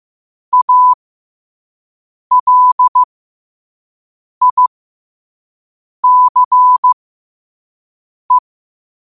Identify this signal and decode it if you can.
Morse code, heard as a keyed tone, about 15 wpm, spaced as for 5 wpm, 1000 Hz. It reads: ALICE
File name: qotdfiles / 5wpm